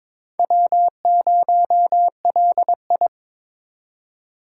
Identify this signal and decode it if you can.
Morse code, heard as a keyed tone, about 22 wpm, 700 Hz. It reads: W0LI
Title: USA Call Signs - 4 Characters - 8x Character Spacing - 22wpm